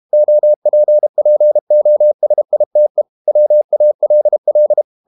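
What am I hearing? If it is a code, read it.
Morse code, heard as a keyed tone, about 32 wpm, 600 Hz. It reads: OPPOSITE WALL
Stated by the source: Morse code practice at 32 wpm from CW med Gnister, reading Ken Follett's novel Never in English.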